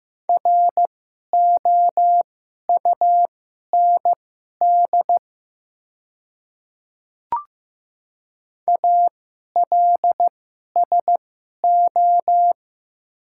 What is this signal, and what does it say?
Morse code, heard as a keyed tone, about 15 wpm, spaced as for 11 wpm, 700 Hz. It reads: ROUND ALSO
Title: 2x Character Spacing - Top 500 Words - 15wpm